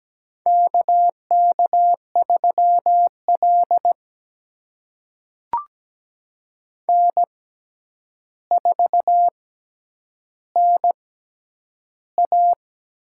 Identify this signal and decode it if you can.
Morse code, heard as a keyed tone, about 17 wpm, spaced as for 9 wpm, 700 Hz. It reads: KK3L N4NA
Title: USA Call Signs - 4 Characters - 6x Character Spacing - 17wpm